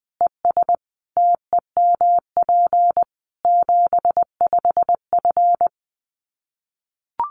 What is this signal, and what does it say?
Morse code, heard as a keyed tone, about 20 wpm, 700 Hz. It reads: ES TEMP 75F
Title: Common QSO Phrases 20wpm